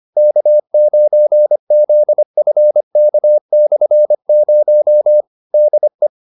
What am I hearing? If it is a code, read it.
Morse code, heard as a keyed tone, about 25 wpm, 600 Hz. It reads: K9ZFK/0 DE
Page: advanced-practice-files